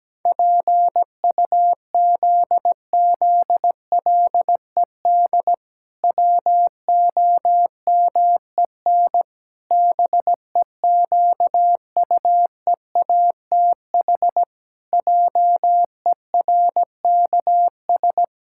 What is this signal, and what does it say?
Morse code, heard as a keyed tone, about 17 wpm, 700 Hz. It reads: PUZZLED WOMEN BEQUEATH JERKS
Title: Pangram Challenge v2 - 17wpm